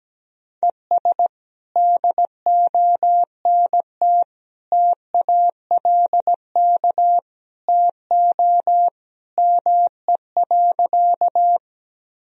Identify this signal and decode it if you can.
Morse code, heard as a keyed tone, about 17 wpm, 700 Hz. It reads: ES DONT TALK TO ME.
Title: Sentences from Top 400 Words and Abbreviations 17wpm